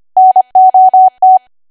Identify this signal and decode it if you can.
Morse code, heard as a keyed tone, about 25 wpm, 750 Hz. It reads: NOT
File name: cwa / W1AW-Level3 / 25WPM